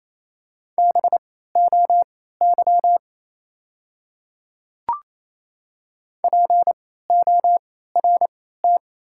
Morse code, heard as a keyed tone, about 28 wpm, spaced as for 16 wpm, 690 Hz. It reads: BOY PORT
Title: 3x Character Spacing - Top 200 Words - 28wpm